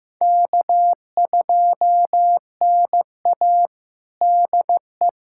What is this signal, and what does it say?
Morse code, heard as a keyed tone, about 15 wpm, 700 Hz. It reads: K2NA DE
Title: USA Callsign DE Callsign - 4 Characters - 15wpm